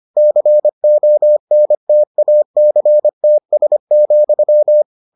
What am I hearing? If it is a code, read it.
Morse code, heard as a keyed tone, about 25 wpm, 600 Hz. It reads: CONTACTS,